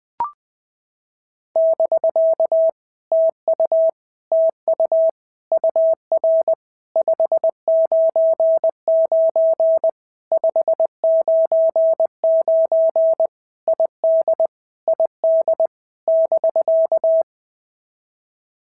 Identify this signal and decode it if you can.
Morse code, heard as a keyed tone, about 20 wpm, 650 Hz. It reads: <BK> TU TU UR 599 599 ID ID <BK>
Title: POTA QSO Practice - 20wpm